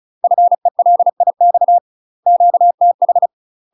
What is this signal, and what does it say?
Morse code, heard as a keyed tone, about 36 wpm, 700 Hz. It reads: FELIX QTH
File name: UK MASTER QSOs TRAINING / 35WPM / UK MASTER QSOs 0032